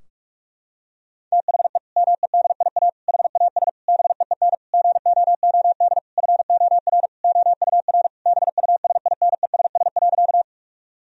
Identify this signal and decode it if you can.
Morse code, heard as a keyed tone, about 46 wpm, 700 Hz. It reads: THE MEDIA HAS BEEN GOOD FOR OUR BUSINESS.
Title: Sentences from Top 1000 Words - Review Part 4 - 45wpm